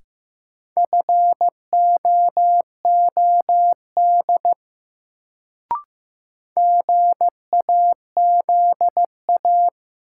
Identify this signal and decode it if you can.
Morse code, heard as a keyed tone, about 15 wpm, 700 Hz. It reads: FOOD GAZA
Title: Sets of 5 Words - New Word Drill - 15wpm